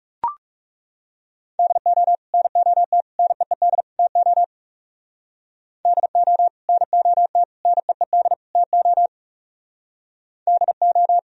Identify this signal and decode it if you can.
Morse code, heard as a keyed tone, about 37 wpm, 700 Hz. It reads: DO NOT NEED TO DO NOT NEED TO DO
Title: Speed-Racing - Sets of 4 Words 30wpm